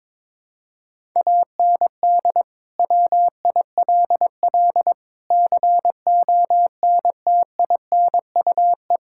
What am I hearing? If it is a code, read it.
Morse code, heard as a keyed tone, about 22 wpm, 700 Hz. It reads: AND WILL CONTINUE
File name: practice / 22wpm / Sets of 6 Words - 22wpm